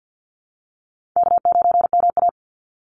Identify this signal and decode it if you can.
Morse code, heard as a keyed tone, about 49 wpm, 700 Hz. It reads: K9MA
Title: Easy Call Signs Repeated 2x - 50wpm